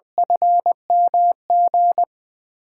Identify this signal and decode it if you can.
Morse code, heard as a keyed tone, about 20 wpm, 700 Hz. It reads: FMG
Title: USA Call Signs - 6 Characters - 8x Character Spacing - 20wpm